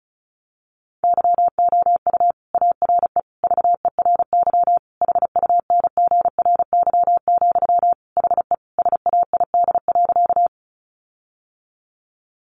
Morse code, heard as a keyed tone, about 34 wpm, 700 Hz. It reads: YOU ARE VERY HUNGRY, HE SAID.
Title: Sentences from Top 1400 Words 35wpm